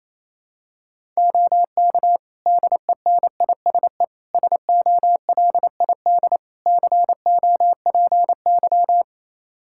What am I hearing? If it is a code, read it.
Morse code, heard as a keyed tone, about 28 wpm, 700 Hz. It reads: OK DENISE SOLID COPY
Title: Practice QSO - Exchange 3 - More Info - 28wpm